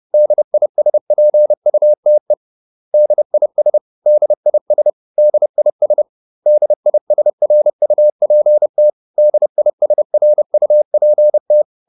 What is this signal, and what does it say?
Morse code, heard as a keyed tone, about 30 wpm, 600 Hz. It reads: DISPUTE DIS DIS DIS DISRUPT DISRUPT